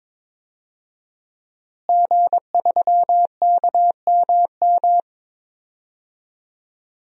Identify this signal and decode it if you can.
Morse code, heard as a keyed tone, about 22 wpm, 700 Hz. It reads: G3KMM
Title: DXCC-100 Call Signs - 5 Characters - Repeated 3x - 22wpm